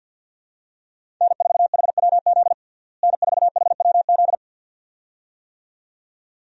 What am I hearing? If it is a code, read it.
Morse code, heard as a keyed tone, about 52 wpm, 700 Hz. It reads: N4HWZ N4HWZ
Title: USA Call Signs - 5 Characters - Repeated 2x - 50wpm